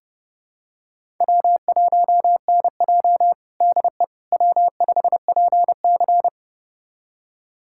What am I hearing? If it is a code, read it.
Morse code, heard as a keyed tone, about 32 wpm, 700 Hz. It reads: W1NJ DE W5PC
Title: USA Callsign DE Callsign - 4 Characters - 30wpm